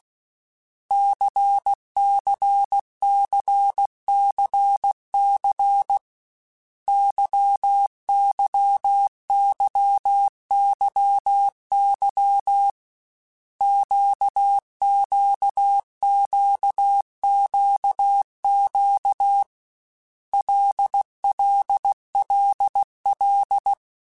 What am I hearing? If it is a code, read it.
Morse code, heard as a keyed tone, about 16 wpm, 800 Hz. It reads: CCCCC YYYYY QQQQQ LLLL
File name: Basic Letter Groups By7Groups / LFCYQ / LFCYQ@13